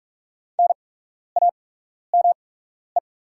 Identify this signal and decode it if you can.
Morse code, heard as a keyed tone, about 44 wpm, spaced as for 12 wpm, 700 Hz. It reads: NAME
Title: 8x Character Spacing - Top 300 Words - 45wpm